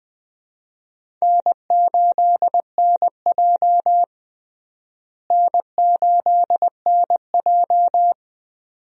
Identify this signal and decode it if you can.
Morse code, heard as a keyed tone, about 20 wpm, 700 Hz. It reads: N8NJ N8NJ
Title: USA Call Signs - 4 Characters - Repeated 2x - 20wpm